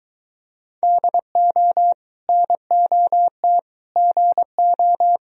Morse code, heard as a keyed tone, about 23 wpm, 700 Hz. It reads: DO NOT GO